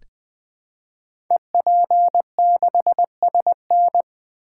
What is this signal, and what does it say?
Morse code, heard as a keyed tone, about 20 wpm, 700 Hz. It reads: EP6SN
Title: Code Groups - 5 Letter:Numbers - 20wpm